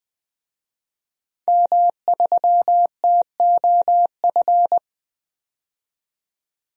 Morse code, heard as a keyed tone, about 20 wpm, 700 Hz. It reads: M3TOF
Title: DXCC-100 Call Signs - 5 Characters - 20wpm